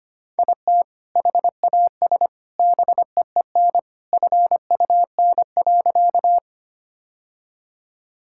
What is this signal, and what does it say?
Morse code, heard as a keyed tone, about 25 wpm, 700 Hz. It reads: IT HAS BEEN FUN.
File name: practice / 25wpm / Sentences with 3-Word Phrases - 25wpm